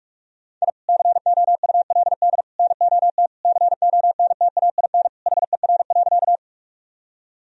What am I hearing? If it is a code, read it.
Morse code, heard as a keyed tone, about 48 wpm, 700 Hz. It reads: I COULD NOT CONTAIN HER.
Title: Sentences from Top 1000 Words - Review Part 2 - 45wpm